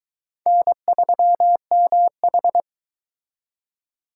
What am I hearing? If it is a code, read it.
Morse code, heard as a keyed tone, about 23 wpm, 700 Hz. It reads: N3MH